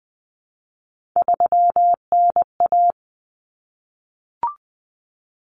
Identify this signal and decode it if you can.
Morse code, heard as a keyed tone, about 20 wpm, 700 Hz. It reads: 3NA
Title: Common Cut Numbers - 3-Digit 20wpm